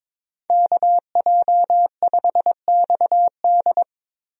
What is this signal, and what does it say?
Morse code, heard as a keyed tone, about 22 wpm, 700 Hz. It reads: KJ5XD
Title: USA Call Signs - 5 Characters - Repeated 2x - 22wpm